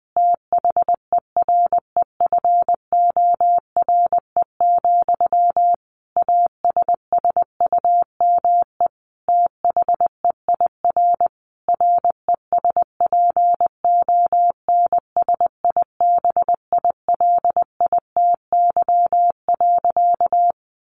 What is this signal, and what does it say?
Morse code, heard as a keyed tone, about 20 wpm, 700 Hz. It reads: THEREFORE, ASSUME THEIR RESPONSIBILITY.
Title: Sentences from Top 1500 Words 20wpm